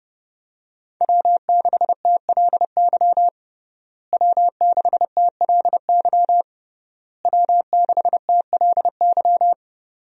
Morse code, heard as a keyed tone, about 30 wpm, 700 Hz. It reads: W6TLY W6TLY W6TLY
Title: USA Call Signs - 5 Characters - Repeated 3x - 30wpm